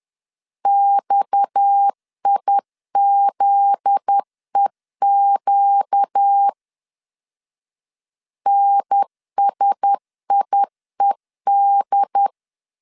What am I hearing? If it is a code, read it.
Morse code, heard as a keyed tone, about 10 wpm, 800 Hz. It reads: XIZEQ NSIED
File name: LEZIONI / 7 GF9Z / 7 GF9Z 10wpm 2611